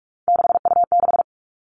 Morse code, heard as a keyed tone, about 44 wpm, 700 Hz. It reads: 6U6
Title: All Cut Numbers - 3-Digit 45wpm